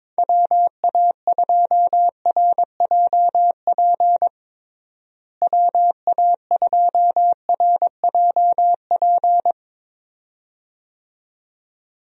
Morse code, heard as a keyed tone, about 22 wpm, 700 Hz. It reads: WA2RJP WA2RJP
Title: USA Call Signs - 6 Characters - Repeated 2x - 22wpm